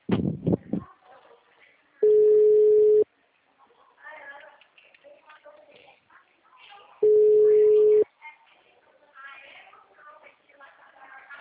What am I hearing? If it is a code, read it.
Ringback tone (Europe, 425 Hz), with speech.
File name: out- -2047- - - .62182